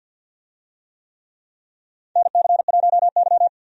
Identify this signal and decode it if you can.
Morse code, heard as a keyed tone, about 53 wpm, 700 Hz. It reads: NC1Y
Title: USA Call Signs - 4 Characters - 50wpm